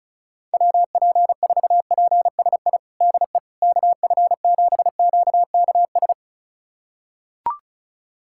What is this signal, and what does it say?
Morse code, heard as a keyed tone, about 36 wpm, 700 Hz. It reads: WP4PSI DE KF7QKS
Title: USA Callsign DE Callsign - 6 Characters - 35wpm